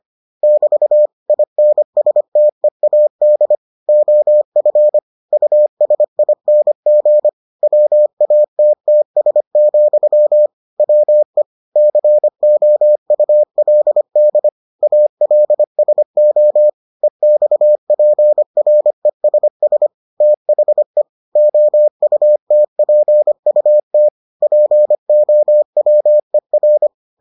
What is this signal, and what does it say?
Morse code, heard as a keyed tone, about 25 wpm, 600 Hz. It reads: <BT> INSTEAD OF USING WATTS, WE COULD ALSO EXPRESS THE OUTPUT POWER